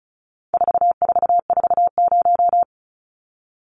Morse code, heard as a keyed tone, about 34 wpm, 700 Hz. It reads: 4440
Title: Numbers - 4-Digit 35wpm